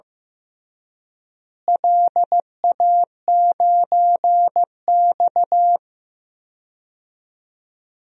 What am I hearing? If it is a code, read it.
Morse code, heard as a keyed tone, about 15 wpm, 700 Hz. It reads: LA9X
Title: DXCC-100 Call Signs - 4 Characters - Repeated 2x - 15wpm